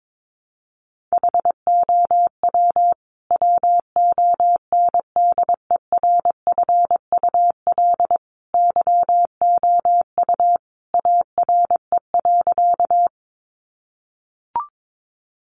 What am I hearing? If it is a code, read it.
Morse code, heard as a keyed tone, about 22 wpm, 700 Hz. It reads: HOW WONDERFUL YOU ARE.
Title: Sentences from Top 1100 Words 22wpm